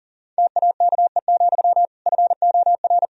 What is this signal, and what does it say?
Morse code, heard as a keyed tone, about 40 wpm, 700 Hz. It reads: TAKE, FOR
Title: Sentences from Top 1000 Words - Review Part 1 - 40wpm